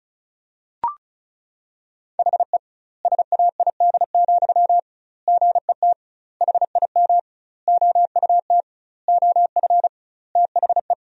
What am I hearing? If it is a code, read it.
Morse code, heard as a keyed tone, about 36 wpm, 700 Hz. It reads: HE SAID, GET HIM OUT OF THE
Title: 2x Word-Spacing - Sentences from Top 400 Words - 35wpm